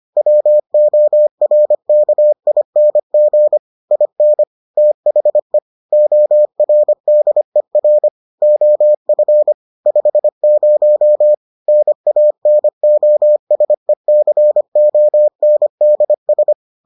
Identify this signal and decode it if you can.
Morse code, heard as a keyed tone, about 25 wpm, 600 Hz. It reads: WORKING IN THE ORDER OF 50 NANOSECONDS